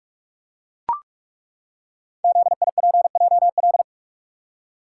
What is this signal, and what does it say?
Morse code, heard as a keyed tone, about 42 wpm, 700 Hz. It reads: ZIPJL
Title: Code Groups - 5 Letter:Numbers - 45wpm